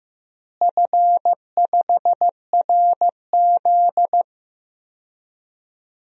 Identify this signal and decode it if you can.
Morse code, heard as a keyed tone, about 15 wpm, 700 Hz. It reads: F5RZ
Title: DXCC-100 Call Signs - 4 Characters - Repeated 3x - 15wpm